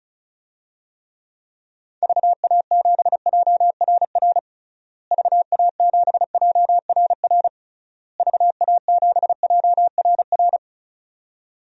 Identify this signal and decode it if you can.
Morse code, heard as a keyed tone, about 36 wpm, 700 Hz. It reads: VA7JRR VA7JRR VA7JRR
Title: DXCC-100 Call Signs - 6 Characters - Repeated 3x - 35wpm